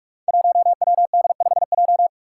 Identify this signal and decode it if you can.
Morse code, heard as a keyed tone, about 42 wpm, 700 Hz. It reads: 1WD5J
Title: Five Challenging Letter-Number Combinations 45wpm